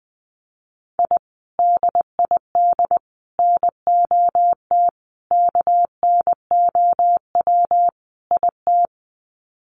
Morse code, heard as a keyed tone, about 19 wpm, 700 Hz. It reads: I DID NOT KNOW IT
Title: Sets of 5 Words Encore2 20wpm